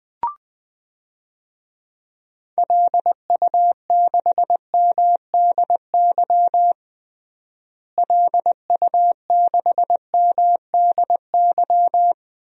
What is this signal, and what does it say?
Morse code, heard as a keyed tone, about 21 wpm, 700 Hz. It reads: LU6MDY LU6MDY
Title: DXCC-100 Call Signs - 6 Characters - Repeated 3x - 20wpm